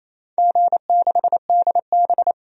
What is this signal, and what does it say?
Morse code, heard as a keyed tone, about 27 wpm, 690 Hz. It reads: G6DB
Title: DXCC-100 Call Signs - 4 Characters - 28wpm